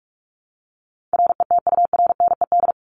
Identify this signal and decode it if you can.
Morse code, heard as a keyed tone, about 44 wpm, 700 Hz. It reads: RETURNED
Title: ED Suffix - Top 100 Words with Base Words 45wpm